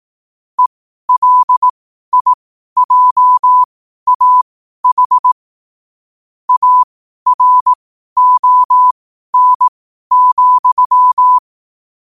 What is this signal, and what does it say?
Morse code, heard as a keyed tone, about 18 wpm, spaced as for 12 wpm, 1000 Hz. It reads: ELIJAH ARON,
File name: qotdfiles / 12wpm